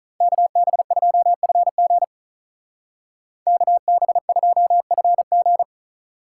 Morse code, heard as a keyed tone, about 44 wpm, 700 Hz. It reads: KB2FG KB2FG